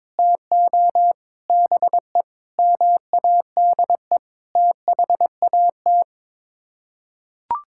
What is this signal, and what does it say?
Morse code, heard as a keyed tone, about 22 wpm, 700 Hz. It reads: TO BE MADE THAT
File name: practice / 22wpm / Sets of 8 Words - 22wpm